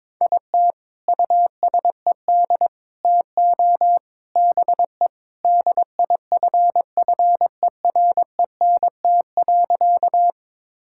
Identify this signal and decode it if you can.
Morse code, heard as a keyed tone, about 23 wpm, 700 Hz. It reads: IT USED TO BE DIFFERENT.
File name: practice / 22wpm / Sentences with 4-Word Phrases - 22wpm